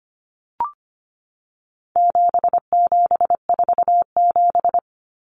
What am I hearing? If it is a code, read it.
Morse code, heard as a keyed tone, about 25 wpm, 700 Hz. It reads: 7747
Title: Common Cut Numbers - 4-Digit 25wpm